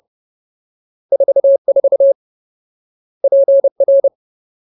Morse code, heard as a keyed tone, about 30 wpm, 550 Hz. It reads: 44 PR
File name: Ultimate Word List / 30WPM Ultimate